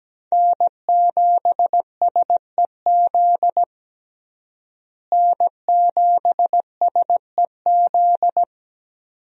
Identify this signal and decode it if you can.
Morse code, heard as a keyed tone, about 17 wpm, 700 Hz. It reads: N7SEZ N7SEZ
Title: USA Call Signs - 5 Characters - Repeated 2x - 17wpm